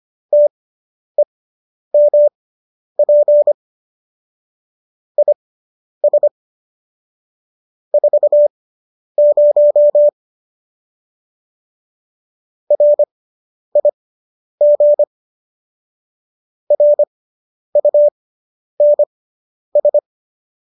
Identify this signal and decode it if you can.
Morse code, heard as a keyed tone, about 25 wpm, spaced as for 10 wpm, 590 Hz. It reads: TEMP IS 40 RIG RUNS